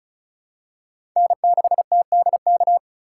Morse code, heard as a keyed tone, about 36 wpm, 700 Hz. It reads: N6TDK
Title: USA Call Signs - 5 Characters - 2x Character Spacing - 35wpm